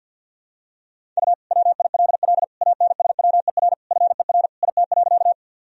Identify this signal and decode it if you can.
Morse code, heard as a keyed tone, about 45 wpm, 700 Hz. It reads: U WILL ANSWER FER IT.
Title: Sentences from Top 300 Words and Abbreviations 50wpm